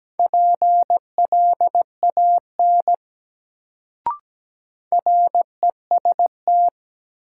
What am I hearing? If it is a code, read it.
Morse code, heard as a keyed tone, about 17 wpm, 700 Hz. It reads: PLAN REST
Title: Sentences from Top 500 Words - New Word Drill - 17wpm